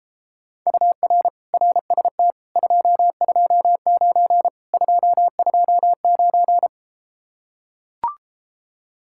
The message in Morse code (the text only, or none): UR RST 229 229